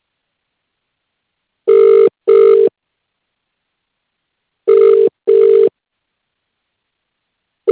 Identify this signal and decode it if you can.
Ringback tone (UK), 400 + 450 Hz.